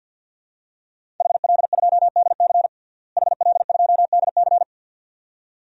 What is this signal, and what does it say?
Morse code, heard as a keyed tone, about 52 wpm, 700 Hz. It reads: HL2DC HL2DC
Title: DXCC-100 Call Signs - 5 Characters - Repeated 2x - 50wpm